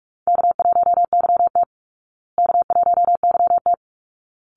Morse code, heard as a keyed tone, about 44 wpm, 700 Hz. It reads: K1YT K1YT
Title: Easy Call Signs Repeated 2x - 45wpm